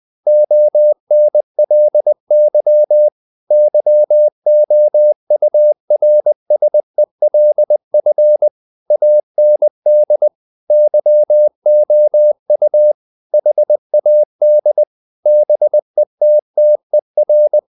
Morse code, heard as a keyed tone, about 20 wpm, 600 Hz. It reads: ONLY YOURSELF AND YOU HAD BETTER
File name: the old man and the sea / the old man and the sea part 1 20wpm